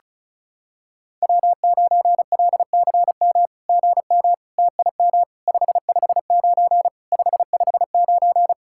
Morse code, heard as a keyed tone, about 36 wpm, 700 Hz. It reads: W9LCM GM TIM 559 559
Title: SOTA QSO Practice 35wpm